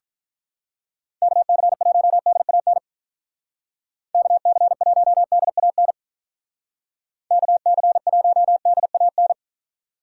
Morse code, heard as a keyed tone, about 43 wpm, 700 Hz. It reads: KC1DAN KC1DAN KC1DAN
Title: Speed Racing - USA Call Signs 35wpm